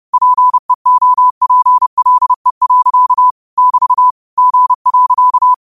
Morse code, heard as a keyed tone, about 30 wpm, 1000 Hz. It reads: PEOPLE. <BT> G.